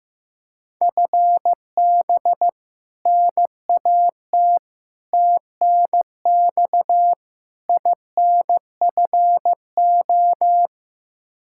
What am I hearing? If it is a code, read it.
Morse code, heard as a keyed tone, about 16 wpm, 700 Hz. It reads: FB NAT TNX INFO
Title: Practice QSO - Exchange 3 - More Info - 15wpm